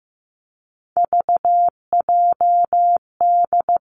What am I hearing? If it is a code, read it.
Morse code, heard as a keyed tone, about 15 wpm, 700 Hz. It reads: VJD
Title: Three Challenging Letter-Number Combinations 15wpm